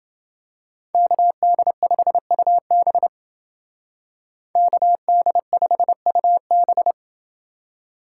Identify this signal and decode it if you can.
Morse code, heard as a keyed tone, about 28 wpm, 700 Hz. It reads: KD5UB KD5UB